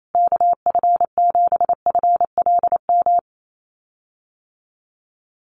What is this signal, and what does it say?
Morse code, heard as a keyed tone, about 28 wpm, 700 Hz. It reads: KF7FLM
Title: USA Call Signs Repeated 3x 28wpm